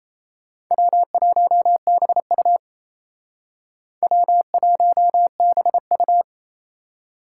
Morse code, heard as a keyed tone, about 36 wpm, 700 Hz. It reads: W1BU W1BU